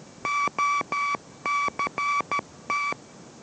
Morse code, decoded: OCT

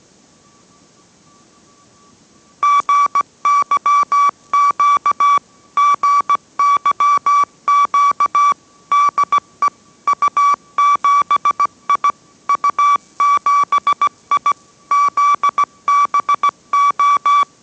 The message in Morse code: GYQ GYQ DE U7I U7I ZBO

